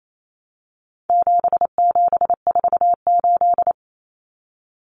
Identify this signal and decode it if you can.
Morse code, heard as a keyed tone, about 28 wpm, 700 Hz. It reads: 7748
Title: Common Cut Numbers - 4-Digit 28wpm